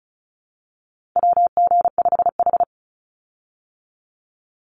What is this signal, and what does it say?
Morse code, heard as a keyed tone, about 34 wpm, 700 Hz. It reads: WG5H
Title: Call Signs - 35wpm